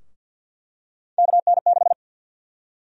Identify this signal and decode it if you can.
Morse code, heard as a keyed tone, about 51 wpm, 700 Hz. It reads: KN6
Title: USA Call Sign Prefixes - 50wpm